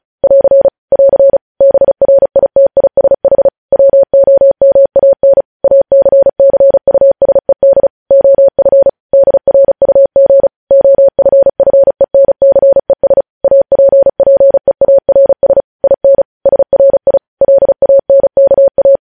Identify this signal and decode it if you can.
Morse code, heard as a keyed tone, about 36 wpm, 550 Hz. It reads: <AR> <AR> BRITISH WOMAN ACCUSED OF DRUG OFFENCES APPEARS IN SRI LANKA
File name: podcast / News headlines-1x-35x35